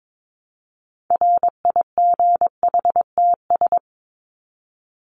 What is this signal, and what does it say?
Morse code, heard as a keyed tone, about 22 wpm, 700 Hz. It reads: RIGHTS
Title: ES:S Suffix - Top 100 Words with Base Words 22wpm